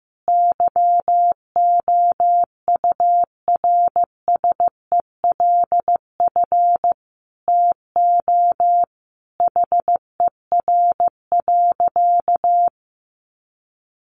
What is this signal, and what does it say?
Morse code, heard as a keyed tone, about 15 wpm, 700 Hz. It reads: YOURSELF TO HER.
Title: Sentences from Top 1100 Words 15wpm